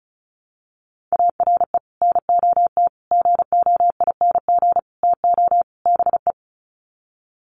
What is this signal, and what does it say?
Morse code, heard as a keyed tone, about 34 wpm, 700 Hz. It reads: ARE NOT GOING TO BE
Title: Sets of 5 Words Encore 35wpm